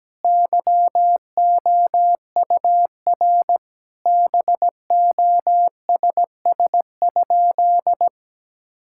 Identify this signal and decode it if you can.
Morse code, heard as a keyed tone, about 17 wpm, 700 Hz. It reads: YOUR BOSS?
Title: Sentences from Top 1000 Words - Review Part 6 - 17wpm